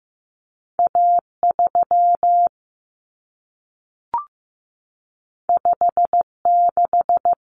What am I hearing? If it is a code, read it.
Morse code, heard as a keyed tone, about 15 wpm, 700 Hz. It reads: A3 56
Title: Common Cut Numbers - 2-Digit 15wpm